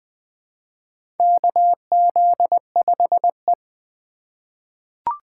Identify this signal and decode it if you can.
Morse code, heard as a keyed tone, about 20 wpm, 700 Hz. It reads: KZ5E
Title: USA Call Signs - 4 Characters - 2x Character Spacing - 20wpm